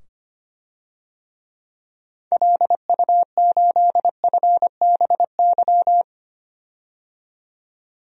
Morse code, heard as a keyed tone, about 25 wpm, 700 Hz. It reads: LU8FBY
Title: DXCC-100 Call Signs - 6 Characters - 25wpm